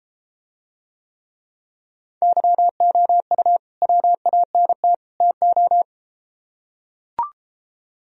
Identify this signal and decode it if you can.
Morse code, heard as a keyed tone, about 33 wpm, 700 Hz. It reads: YOU WANT TO